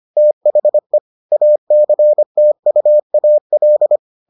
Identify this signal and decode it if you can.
Morse code, heard as a keyed tone, about 25 wpm, 600 Hz. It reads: THE ACTUAL